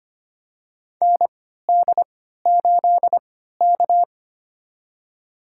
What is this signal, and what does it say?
Morse code, heard as a keyed tone, about 25 wpm, spaced as for 14 wpm, 700 Hz. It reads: ND8K
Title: USA Call Signs - 3x Character Spacing - 25wpm